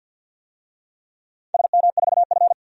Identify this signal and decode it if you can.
Morse code, heard as a keyed tone, about 47 wpm, 700 Hz. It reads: SM4F
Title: DXCC-100 Call Signs - 4 Characters - Repeated 2x - 50wpm